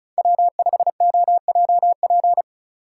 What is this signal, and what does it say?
Morse code, heard as a keyed tone, about 36 wpm, 700 Hz. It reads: W5OJP
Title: USA Call Signs - 5 Characters - 4x Character Spacing - 35wpm